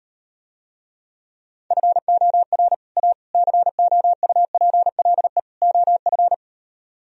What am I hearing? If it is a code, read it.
Morse code, heard as a keyed tone, about 38 wpm, 700 Hz. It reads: FOR A COUPLE OF